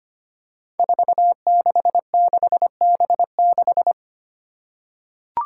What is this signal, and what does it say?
Morse code, heard as a keyed tone, about 25 wpm, 700 Hz. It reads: 466B6
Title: Five Challenging Letter-Number Combinations 25wpm